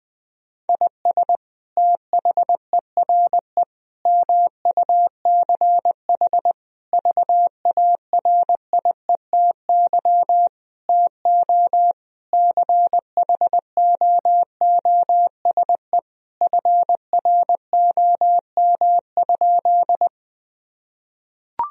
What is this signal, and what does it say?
Morse code, heard as a keyed tone, about 20 wpm, 700 Hz. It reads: IS THERE MUCH VARIETY TO CHOOSE FROM?
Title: Sentences from Top 1000 Words - Review Part 4 - 20wpm